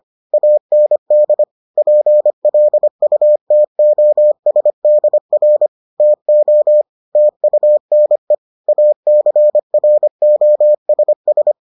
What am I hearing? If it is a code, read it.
Morse code, heard as a keyed tone, about 25 wpm, 600 Hz. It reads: AND PLUTOSDR TO TUNE ACROSS